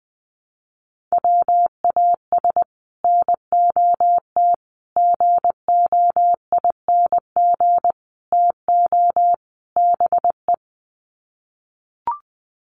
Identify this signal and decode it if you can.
Morse code, heard as a keyed tone, about 20 wpm, 700 Hz. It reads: WAS NOT GOING TO BE
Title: Sets of 5 Words Encore 20wpm